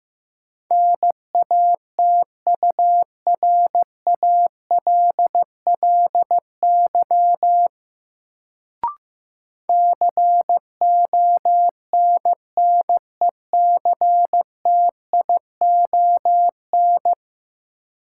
Morse code, heard as a keyed tone, about 15 wpm, 700 Hz. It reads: NATURALLY CONNECTION
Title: Sentences from Top 1300 Words - New Word Drill - 15wpm